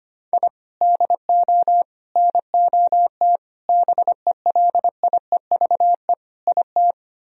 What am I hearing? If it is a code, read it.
Morse code, heard as a keyed tone, about 31 wpm, 700 Hz. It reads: I DO NOT BELIEVE IT